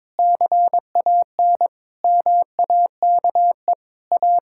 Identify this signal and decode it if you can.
Morse code, heard as a keyed tone, about 22 wpm, 700 Hz. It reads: CAN MAKE A
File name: Difficult Sentences with Hints - 22wpm